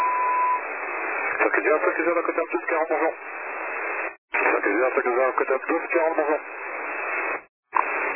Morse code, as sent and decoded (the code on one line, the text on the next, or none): none